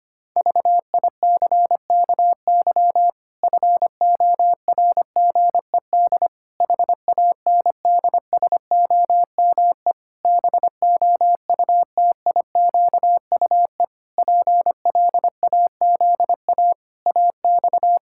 Morse code, heard as a keyed tone, about 25 wpm, 700 Hz. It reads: VICKY FORGED HANDSOME BOUTIQUE PLAZA AX
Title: Pangram Challenge v2 - 25wpm